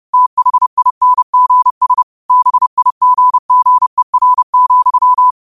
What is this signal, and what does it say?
Morse code, heard as a keyed tone, about 30 wpm, 1000 Hz. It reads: THINGS BIGGER,
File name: qotdfiles / 30wpm